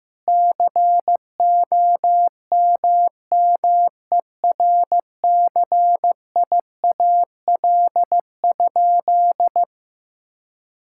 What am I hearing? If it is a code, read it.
Morse code, heard as a keyed tone, about 16 wpm, 700 Hz. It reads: COMMERCIAL?
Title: Sentences from Top 1000 Words - Review Part 5 - 15wpm